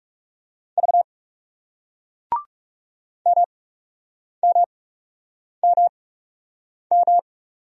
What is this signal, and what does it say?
Morse code, heard as a keyed tone, about 39 wpm, spaced as for 8 wpm, 700 Hz. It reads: V MMMM